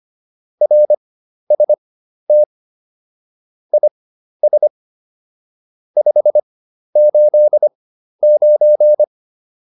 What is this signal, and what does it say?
Morse code, heard as a keyed tone, about 25 wpm, spaced as for 12 wpm, 600 Hz. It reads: RST IS 589
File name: QSO 203 12